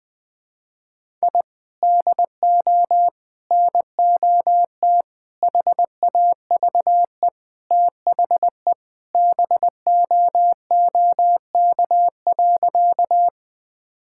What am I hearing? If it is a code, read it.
Morse code, heard as a keyed tone, about 20 wpm, 700 Hz. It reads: I DO NOT HAVE THE BOOK.
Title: Sentences with 4-Word Phrases - Spoken Beginning - 20wpm